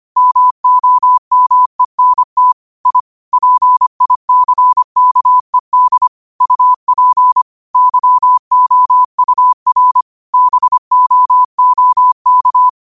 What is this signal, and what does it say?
Morse code, heard as a keyed tone, about 25 wpm, 1000 Hz. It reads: MOMENT I PICKED UP YOUR BOOK